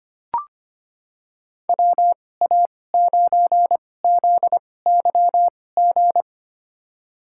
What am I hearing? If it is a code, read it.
Morse code, heard as a keyed tone, about 25 wpm, spaced as for 18 wpm, 700 Hz. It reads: WA9ZYG
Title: USA Call Signs - 2x Character Spacing - 25wpm